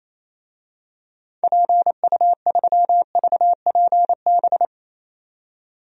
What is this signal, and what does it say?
Morse code, heard as a keyed tone, about 27 wpm, 700 Hz. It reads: PU3VPB
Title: DXCC-100 Call Signs - 6 Characters - 28wpm